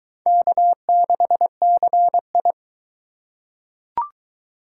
Morse code, heard as a keyed tone, about 20 wpm, 700 Hz. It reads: K6CI